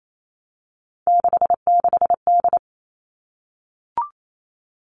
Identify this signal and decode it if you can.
Morse code, heard as a keyed tone, about 28 wpm, 700 Hz. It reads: 66D
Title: All Cut Numbers - 3-Digit 28wpm